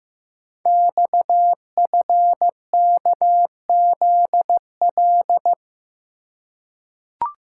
Morse code, heard as a keyed tone, about 15 wpm, 700 Hz. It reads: XFKZL
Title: Code Groups - 5 Letters - 15wpm